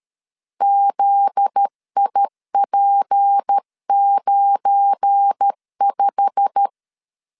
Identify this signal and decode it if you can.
Morse code, heard as a keyed tone, about 13 wpm, 800 Hz. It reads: ZIP95
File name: LEZIONI / 10 SIGNS / 10B INT 12wpm 2414